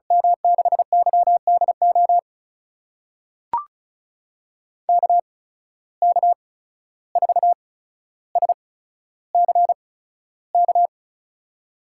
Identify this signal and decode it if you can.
Morse code, heard as a keyed tone, about 36 wpm, spaced as for 14 wpm, 700 Hz. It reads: M6YDO KK4SCK
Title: USA Call Signs - 6 Characters - 8x Character Spacing - 35wpm